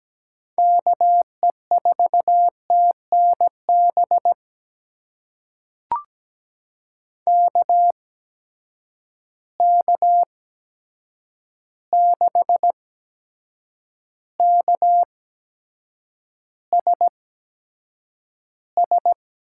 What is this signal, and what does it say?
Morse code, heard as a keyed tone, about 17 wpm, spaced as for 7 wpm, 700 Hz. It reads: KE4TNB KK6KSS
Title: USA Call Signs - 6 Characters - 8x Character Spacing - 17wpm